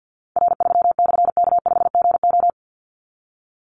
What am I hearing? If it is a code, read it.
Morse code, heard as a keyed tone, about 50 wpm, 700 Hz. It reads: R3/K5GO
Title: Call Signs - 50wpm